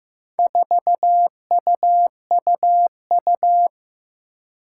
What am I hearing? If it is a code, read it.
Morse code, heard as a keyed tone, about 15 wpm, 700 Hz. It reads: 4UUU
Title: Four Challenging Letter-Number Combinations 15wpm